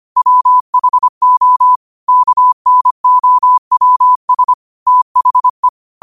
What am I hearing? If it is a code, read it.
Morse code, heard as a keyed tone, about 25 wpm, 1000 Hz. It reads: WHO KNOWS THE